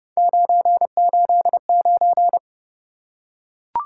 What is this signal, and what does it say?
Morse code, heard as a keyed tone, about 29 wpm, 700 Hz. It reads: 989